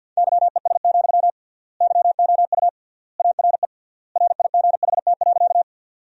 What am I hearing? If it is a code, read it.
Morse code, heard as a keyed tone, about 48 wpm, 700 Hz. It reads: YES, YOU ARE RIGHT.
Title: 3x Word-Spacing - Sentences from Top 500 Words - 50wpm